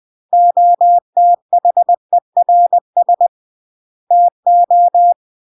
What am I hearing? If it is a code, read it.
Morse code, heard as a keyed tone, about 20 wpm, 700 Hz. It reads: OTHERS TO
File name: CW / jodorowsky / 82 commandments 20-20